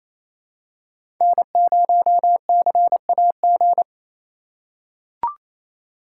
Morse code, heard as a keyed tone, about 27 wpm, 700 Hz. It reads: N0CAG
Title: USA Call Signs - 5 Characters - 4x Character Spacing - 28wpm